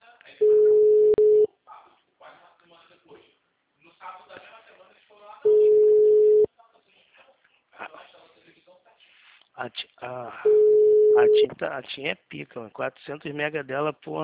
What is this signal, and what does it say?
Ringback tone (Europe, 425 Hz), with speech.
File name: out- -2007- - - .113